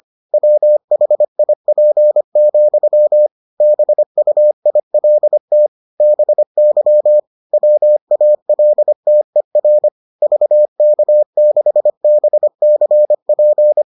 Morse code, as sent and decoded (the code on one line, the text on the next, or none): .-- .... .. .--. --..-- / -... ..- .. .-.. - / -... -.-- / .-- .- .-.. - . .-. / ...- -.- -.... -... -.-. .--.
WHIP, BUILT BY WALTER VK6BCP